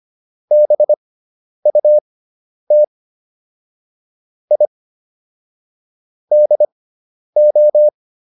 Morse code, heard as a keyed tone, about 25 wpm, spaced as for 10 wpm, 600 Hz. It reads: BUT I DO